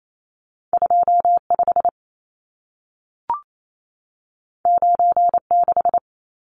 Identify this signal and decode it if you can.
Morse code, heard as a keyed tone, about 28 wpm, 700 Hz. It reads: 25 96
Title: Numbers - 2-Digit 28wpm